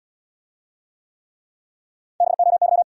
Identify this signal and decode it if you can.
Morse code, heard as a keyed tone, about 79 wpm, 700 Hz. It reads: BOY